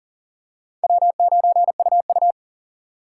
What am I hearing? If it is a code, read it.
Morse code, heard as a keyed tone, about 40 wpm, 700 Hz. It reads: W9UU
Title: USA Call Signs - 3x Character Spacing - 40wpm